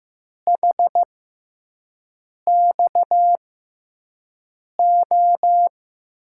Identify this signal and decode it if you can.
Morse code, heard as a keyed tone, about 15 wpm, spaced as for 5 wpm, 700 Hz. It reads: HXO
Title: USA Call Signs - 5 Characters - 6x Character Spacing - 15wpm